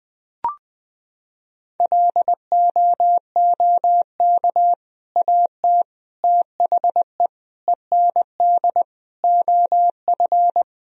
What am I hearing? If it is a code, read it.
Morse code, heard as a keyed tone, about 20 wpm, 700 Hz. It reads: LOOK AT THE END OF
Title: Sentences from Top 1000 Words - Review Part 1 - 20wpm